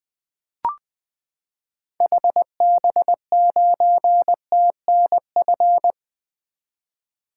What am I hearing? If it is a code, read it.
Morse code, heard as a keyed tone, about 20 wpm, 700 Hz. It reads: HB9TNF
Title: 2023 SOTA Activator Call Signs - 20wpm